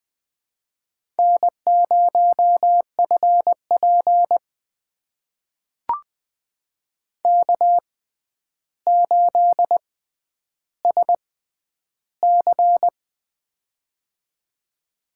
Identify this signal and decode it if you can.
Morse code, heard as a keyed tone, about 20 wpm, spaced as for 10 wpm, 700 Hz. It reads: N0FP K8SC
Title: USA Call Signs - 4 Characters - 6x Character Spacing - 20wpm